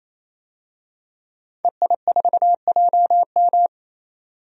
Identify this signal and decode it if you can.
Morse code, heard as a keyed tone, about 28 wpm, 700 Hz. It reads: EI4JM
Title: DXCC-100 Call Signs - 5 Characters - 28wpm